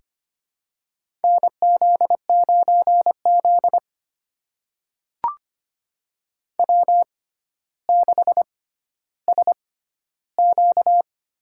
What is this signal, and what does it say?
Morse code, heard as a keyed tone, about 25 wpm, spaced as for 13 wpm, 700 Hz. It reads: NZ9Z W6SQ
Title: USA Call Signs - 4 Characters - 6x Character Spacing - 25wpm